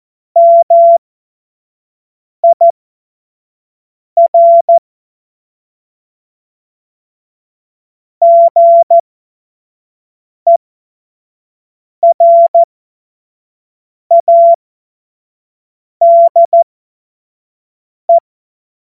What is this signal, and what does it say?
Morse code, heard as a keyed tone, about 14 wpm, spaced as for 5 wpm, 680 Hz. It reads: MIR GERADE
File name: Morsekurs / Witze / Witz 11. Juli 2023 25 F70